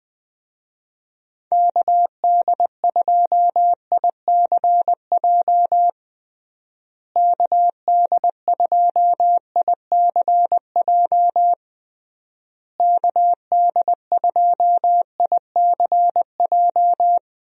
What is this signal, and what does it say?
Morse code, heard as a keyed tone, about 21 wpm, 700 Hz. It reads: KD2ICJ KD2ICJ KD2ICJ
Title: USA Call Signs - 6 Characters - Repeated 3x - 20wpm